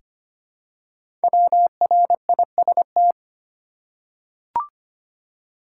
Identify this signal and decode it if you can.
Morse code, heard as a keyed tone, about 25 wpm, 700 Hz. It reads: WRIST
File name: 5 Letter Words - First 3 Letters Given - 25wpm